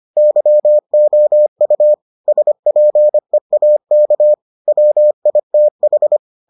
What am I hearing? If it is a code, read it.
Morse code, heard as a keyed tone, about 25 wpm, 600 Hz. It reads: YOU SPEAK WITH